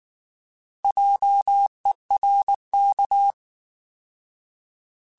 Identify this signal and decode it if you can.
Morse code, heard as a keyed tone, about 19 wpm, 800 Hz. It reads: JERK
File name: Alphabetical / Alphabetical mp3s / Js @18